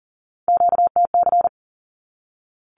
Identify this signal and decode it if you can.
Morse code, heard as a keyed tone, about 40 wpm, 700 Hz. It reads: QTC
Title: Q-Codes 40wpm